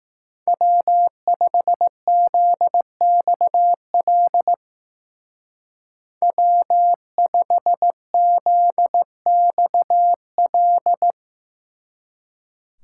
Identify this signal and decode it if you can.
Morse code, heard as a keyed tone, about 18 wpm, 700 Hz. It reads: W5ZXL W5ZXL
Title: Speed Racing - USA 1x3 Call Signs 15wpm